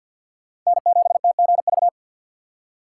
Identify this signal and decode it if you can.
Morse code, heard as a keyed tone, about 50 wpm, 700 Hz. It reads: N7TGV
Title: USA Call Signs - 2x Character Spacing - 50wpm